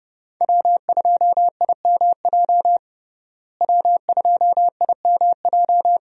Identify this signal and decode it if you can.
Morse code, heard as a keyed tone, about 30 wpm, 700 Hz. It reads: W2IMJ W2IMJ
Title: USA Call Signs - 5 Characters - Repeated 3x - 30wpm